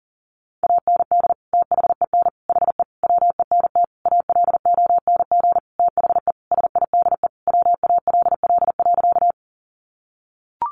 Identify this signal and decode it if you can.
Morse code, heard as a keyed tone, about 40 wpm, 700 Hz. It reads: AND THEN HE WENT ALONG THE SIDE WALL.
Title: Sentences from Top 1100 Words 40wpm